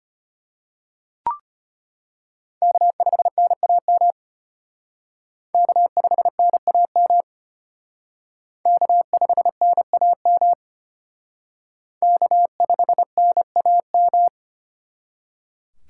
Morse code, heard as a keyed tone, about 32 wpm, 700 Hz. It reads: K5NAM K5NAM K5NAM K5NAM